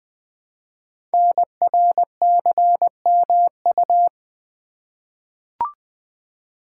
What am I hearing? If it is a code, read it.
Morse code, heard as a keyed tone, about 20 wpm, 700 Hz. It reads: NRCMU
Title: Code Groups - 5 Letters - 20wpm